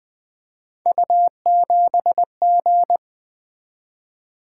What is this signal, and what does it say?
Morse code, heard as a keyed tone, about 20 wpm, 700 Hz. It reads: U7G
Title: Three Letter-Number Combinations - 20wpm